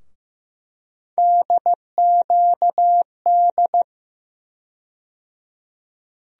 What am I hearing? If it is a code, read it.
Morse code, heard as a keyed tone, about 15 wpm, 700 Hz. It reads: DQD
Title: USA Call Sign Postfixes - 3 Letters 15wpm